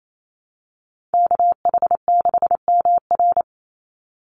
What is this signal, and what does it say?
Morse code, heard as a keyed tone, about 28 wpm, 700 Hz. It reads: KH6MR
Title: USA Call Signs 28wpm